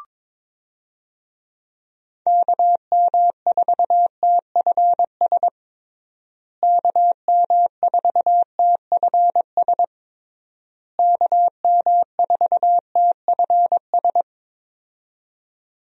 Morse code, heard as a keyed tone, about 22 wpm, 700 Hz. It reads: KM4TFS KM4TFS KM4TFS
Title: USA Call Signs - 6 Characters - Repeated 3x - 22wpm